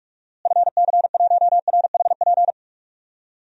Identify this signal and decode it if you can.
Morse code, heard as a keyed tone, about 46 wpm, 700 Hz. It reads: FC1RHP
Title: DXCC-100 Call Signs - 6 Characters - Repeated 3x - 45wpm